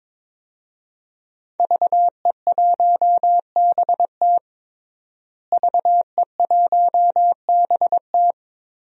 Morse code, heard as a keyed tone, about 21 wpm, 700 Hz. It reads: VE1BT VE1BT
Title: DXCC-100 Call Signs - 5 Characters - Repeated 2x - 22wpm